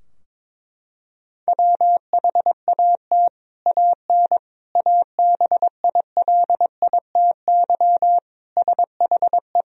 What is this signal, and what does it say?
Morse code, heard as a keyed tone, about 21 wpm, 700 Hz. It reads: WHAT AN ABILITY SHE
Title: Sentences from Top 1000 Words - Review Part 3 - 22wpm